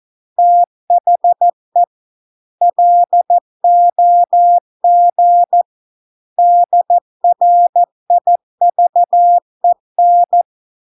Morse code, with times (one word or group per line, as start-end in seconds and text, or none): 0.4-1.8 THE
2.6-5.6 LOG
6.4-10.4 DRIVEN